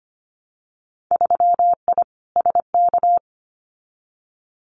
Morse code, heard as a keyed tone, about 25 wpm, 700 Hz. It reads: 3I SK
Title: Winter Field Day Exchanges Encore 25wpm